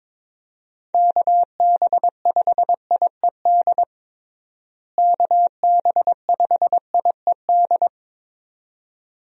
Morse code, heard as a keyed tone, about 22 wpm, 700 Hz. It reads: KB5IED KB5IED
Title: USA Call Signs - 6 Characters - Repeated 2x - 22wpm